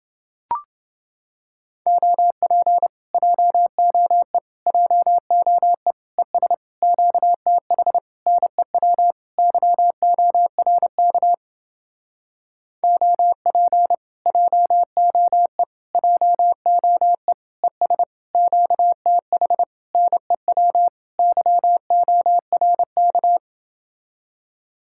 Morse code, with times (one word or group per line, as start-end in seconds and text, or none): 1.9-2.9 OP
3.1-4.4 JOE
4.7-5.9 JOE
6.2-6.5 ES
6.8-8.0 QTH
8.3-9.1 NEW
9.4-11.3 YORK
12.8-13.9 OP
14.3-15.6 JOE
15.9-17.3 JOE
17.6-18.0 ES
18.3-19.6 QTH
19.9-20.9 NEW
21.2-23.4 YORK